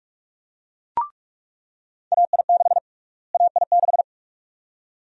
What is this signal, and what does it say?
Morse code, heard as a keyed tone, about 46 wpm, 700 Hz. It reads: AI6 AI6
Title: USA Call Sign Prefixes - Repeated 2x - 45wpm